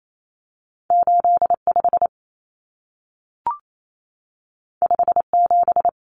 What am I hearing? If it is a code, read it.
Morse code, heard as a keyed tone, about 28 wpm, 700 Hz. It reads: 85 57
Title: Common Cut Numbers - 2-Digit 28wpm